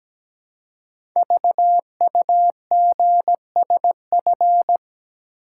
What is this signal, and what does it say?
Morse code, heard as a keyed tone, about 17 wpm, 700 Hz. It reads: VUGSF
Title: Code Groups - 5 Letters - 17wpm